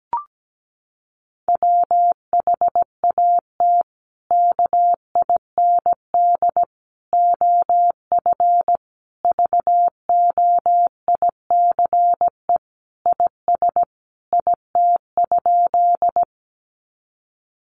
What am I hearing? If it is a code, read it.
Morse code, heard as a keyed tone, about 17 wpm, 700 Hz. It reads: WHAT KIND OF VOICE IS IT?
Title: Sentences from Top 500 Words 17wpm